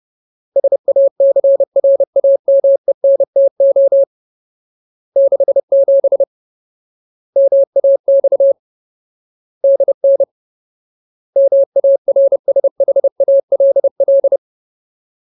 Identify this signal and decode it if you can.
Morse code, heard as a keyed tone, about 30 wpm, 550 Hz. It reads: SACRAMENTO 67 MAX DN MARSHALL
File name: Ultimate Word List / 30WPM Ultimate